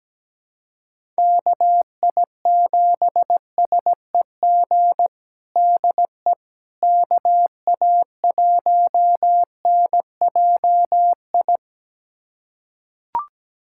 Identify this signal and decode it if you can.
Morse code, heard as a keyed tone, about 17 wpm, 700 Hz. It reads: KI7SEG DE KA1NJI
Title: USA Callsign DE Callsign - 6 Characters - 17wpm